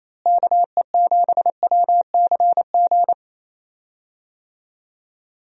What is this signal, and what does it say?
Morse code, heard as a keyed tone, about 27 wpm, 700 Hz. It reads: KE7WCG
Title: USA Call Signs - 6 Characters - Repeated 3x - 28wpm